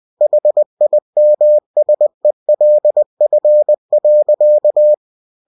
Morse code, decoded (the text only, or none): HIMSELF.